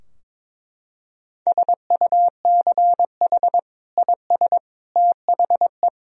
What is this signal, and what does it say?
Morse code, heard as a keyed tone, about 22 wpm, 700 Hz. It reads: SUCH IS THE
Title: Sentences from Top 1000 Words - Review Part 3 - 22wpm